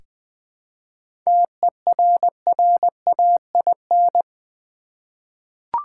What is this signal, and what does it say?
Morse code, heard as a keyed tone, about 20 wpm, 700 Hz. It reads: TERRAIN
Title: 7 Letter Words - First 3 Letters Given - 20wpm